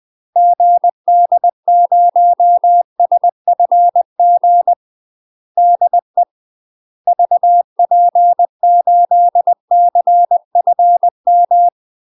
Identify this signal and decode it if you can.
Morse code, heard as a keyed tone, about 20 wpm, 700 Hz. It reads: GD0SFG DE VP8CFM